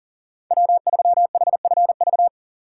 Morse code, heard as a keyed tone, about 40 wpm, 700 Hz. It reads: W3HFV
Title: USA Call Signs - 6x Character Spacing - 40wpm